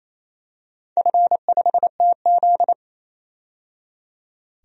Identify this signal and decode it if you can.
Morse code, heard as a keyed tone, about 28 wpm, 700 Hz. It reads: F5TZ
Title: DXCC-100 Call Signs - 4 Characters - 28wpm